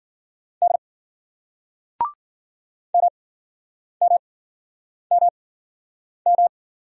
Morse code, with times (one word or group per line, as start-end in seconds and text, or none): 0.6-0.8 D
2.9-6.5 MMMM